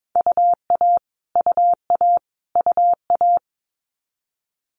Morse code, heard as a keyed tone, about 22 wpm, 700 Hz. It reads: UA UA UA
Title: Multi-letter Phonemes Repeated 3x 22wpm